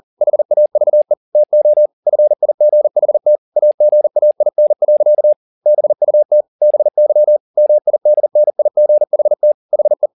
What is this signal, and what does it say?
Morse code, heard as a keyed tone, about 40 wpm, 600 Hz. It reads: HAVE TO FIGHT AGAIN. BUT BY MIDNIGHT HE
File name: the old man and the sea 2 40wpm